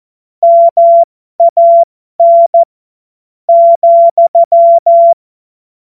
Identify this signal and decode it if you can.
Morse code, heard as a keyed tone, about 14 wpm, spaced as for 10 wpm, 680 Hz. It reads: MAN,
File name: Morsekurs / Witze